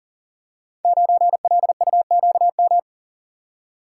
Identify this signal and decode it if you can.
Morse code, heard as a keyed tone, about 39 wpm, 690 Hz. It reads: 9LUQM
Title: Code Groups - 5 Letter:Numbers - 40wpm